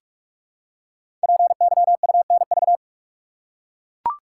Morse code, heard as a keyed tone, about 44 wpm, 700 Hz. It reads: PYUNV
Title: Code Groups - 5 Letters - 45wpm